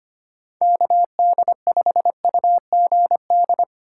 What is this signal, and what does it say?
Morse code, heard as a keyed tone, about 25 wpm, 700 Hz. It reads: KD5UGD
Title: USA Call Signs - 6 Characters - 3x Character Spacing - 25wpm